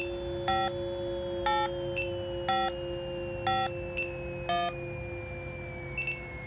DTMF, heard as telephone touch tones, tones of about 210 ms. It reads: 5 8 5 5 1